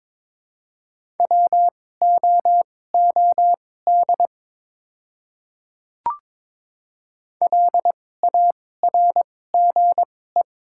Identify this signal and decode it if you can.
Morse code, heard as a keyed tone, about 23 wpm, spaced as for 16 wpm, 700 Hz. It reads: WOOD LARGE
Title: 2x Character Spacing - Top 300 Words - 22wpm